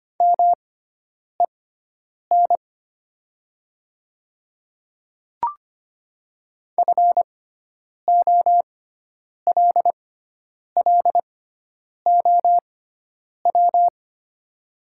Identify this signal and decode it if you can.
Morse code, heard as a keyed tone, about 25 wpm, spaced as for 9 wpm, 700 Hz. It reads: MEN FOLLOW
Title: 6x Character Spacing - Top 300 Words - 25wpm